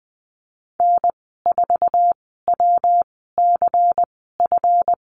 Morse code, heard as a keyed tone, about 20 wpm, spaced as for 14 wpm, 700 Hz. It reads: N4WCF
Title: 238 Lesson 38g - Letter q - Call Signs 20wpm